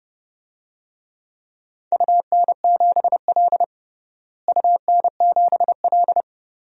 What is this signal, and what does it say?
Morse code, heard as a keyed tone, about 30 wpm, 700 Hz. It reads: UN7L UN7L
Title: DXCC-100 Call Signs - 4 Characters - Repeated 2x - 30wpm